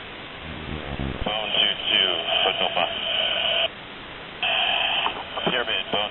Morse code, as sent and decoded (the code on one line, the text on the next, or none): none